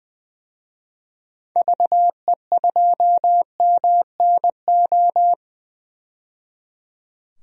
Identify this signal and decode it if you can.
Morse code, heard as a keyed tone, about 20 wpm, 700 Hz. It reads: VE2MNO
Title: DXCC-100 Call Signs - 6 Characters - 20wpm